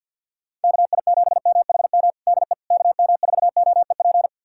Morse code, heard as a keyed tone, about 47 wpm, 700 Hz. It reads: KI7MHM DE KM4OEP
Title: USA Callsign DE Callsign - 6 Characters - 50wpm